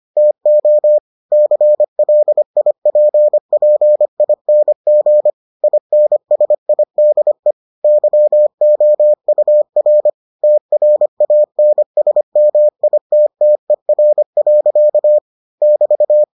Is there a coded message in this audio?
Yes: Morse code from 0.2 to 16.2 s, heard as a keyed tone, about 25 wpm, 600 Hz. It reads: TO CLIPPING INSIDE YOUR TRANSMITTER. <BT>